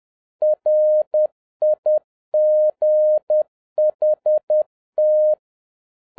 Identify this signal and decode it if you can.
Morse code, heard as a keyed tone, about 10 wpm, 610 Hz. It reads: RIGHT